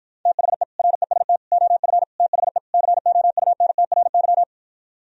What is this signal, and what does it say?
Morse code, heard as a keyed tone, about 52 wpm, 700 Hz. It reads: THE REST OF THE COUNTRY